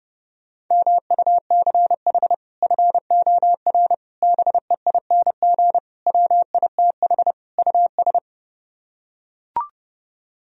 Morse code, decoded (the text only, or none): MUCH FOR BEING WITH US